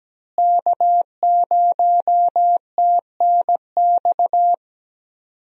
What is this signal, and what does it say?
Morse code, heard as a keyed tone, about 17 wpm, 700 Hz. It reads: K0TNX
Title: USA Call Signs - 5 Characters - Repeated 3x - 17wpm